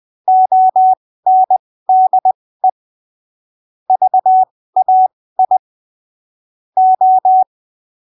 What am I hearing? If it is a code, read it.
Morse code, heard as a keyed tone, about 20 wpm, spaced as for 15 wpm, 750 Hz. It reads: ONDE VAI O
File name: arquivos / texto / texto 13 20ppm